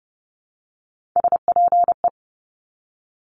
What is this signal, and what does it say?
Morse code, heard as a keyed tone, about 30 wpm, 700 Hz. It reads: SPE
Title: Top Three Letter Combinations in Words 30wpm